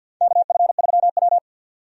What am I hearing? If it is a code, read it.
Morse code, heard as a keyed tone, about 53 wpm, 700 Hz. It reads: KF3W
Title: USA Call Signs - 4 Characters - 50wpm